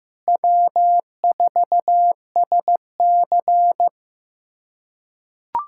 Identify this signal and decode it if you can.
Morse code, heard as a keyed tone, about 15 wpm, 700 Hz. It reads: W4SC
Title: USA Callsign DE Callsign - 4 Characters - 15wpm